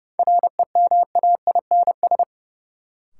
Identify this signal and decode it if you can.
Morse code, heard as a keyed tone, about 30 wpm, 700 Hz. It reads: REMAINS
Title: Sets of 5 Words Encore - New Word Drill - 30wpm